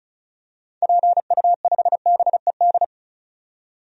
Morse code, heard as a keyed tone, about 34 wpm, 700 Hz. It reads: PU5BED
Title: DXCC-100 Call Signs - 6 Characters - 35wpm